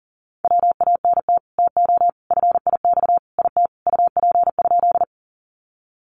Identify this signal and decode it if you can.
Morse code, heard as a keyed tone, about 40 wpm, 700 Hz. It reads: WANT TO FIX IT UP?
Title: Sentences from Top 1400 Words 40wpm